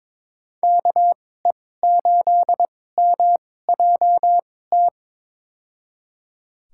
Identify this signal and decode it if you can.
Morse code, heard as a keyed tone, about 22 wpm, spaced as for 16 wpm, 700 Hz. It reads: KE8MJT
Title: USA Call Signs - 2x Character Spacing - 22wpm